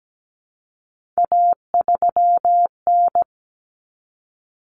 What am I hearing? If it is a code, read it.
Morse code, heard as a keyed tone, about 17 wpm, 700 Hz. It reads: A3N
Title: Common Cut Numbers - 3-Digit 17wpm